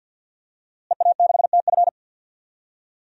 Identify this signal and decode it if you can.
Morse code, heard as a keyed tone, about 48 wpm, 700 Hz. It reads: EA6TF
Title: DXCC-100 Call Signs - 5 Characters - Repeated 3x - 50wpm